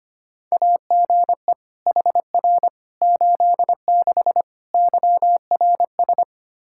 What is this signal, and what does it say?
Morse code, heard as a keyed tone, about 25 wpm, 700 Hz. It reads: AGE HR 86 YRS
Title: Practice QSO - Exchange 3 - More Info - 25wpm